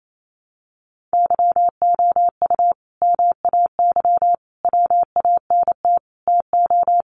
Morse code, heard as a keyed tone, about 28 wpm, 700 Hz. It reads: YOU MAY WANT TO
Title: Sentences from Top 700 Words 28wpm